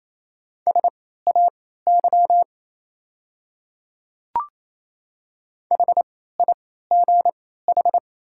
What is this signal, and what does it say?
Morse code, heard as a keyed tone, about 28 wpm, spaced as for 16 wpm, 700 Hz. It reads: SAY HIGH
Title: 3x Character Spacing - Top 300 Words - 28wpm